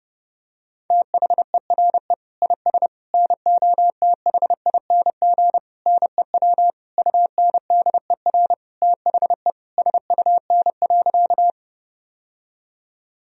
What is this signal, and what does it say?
Morse code, heard as a keyed tone, about 32 wpm, 700 Hz. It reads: THERE IS NOTHING NEW UNDER THE SUN.
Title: Sentences from Top 1000 Words - Review Part 1 - 30wpm